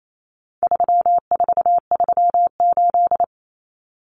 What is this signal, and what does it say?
Morse code, heard as a keyed tone, about 28 wpm, 700 Hz. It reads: 3438
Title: Numbers - 4-Digit 28wpm